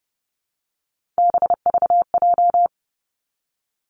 Morse code, heard as a keyed tone, about 30 wpm, 700 Hz. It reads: BVJ
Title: Three Challenging Letter-Number Combinations 30wpm